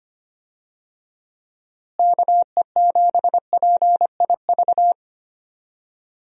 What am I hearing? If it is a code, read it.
Morse code, heard as a keyed tone, about 25 wpm, 700 Hz. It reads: KE7PIV
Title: USA Call Signs - 6 Characters - Repeated 3x - 25wpm